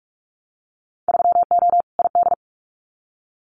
Morse code, heard as a keyed tone, about 46 wpm, 700 Hz. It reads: 3O ID
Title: Winter Field Day Exchanges Encore 45wpm